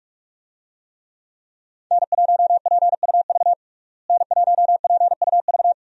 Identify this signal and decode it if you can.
Morse code, heard as a keyed tone, about 41 wpm, 700 Hz. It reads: N1PUV N1PUV
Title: USA Call Signs - 5 Characters - Repeated 2x - 45wpm